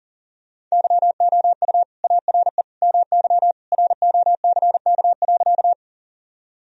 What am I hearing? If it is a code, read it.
Morse code, heard as a keyed tone, about 37 wpm, 700 Hz. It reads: YOU ARE MY ROCK.
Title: Sentences from Top 1000 Words - Review Part 2 - 40wpm